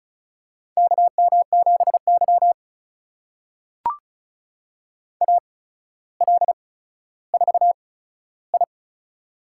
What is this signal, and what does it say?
Morse code, heard as a keyed tone, about 36 wpm, spaced as for 15 wpm, 700 Hz. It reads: KM7Y AL4I
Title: USA Call Signs - 4 Characters - 8x Character Spacing - 35wpm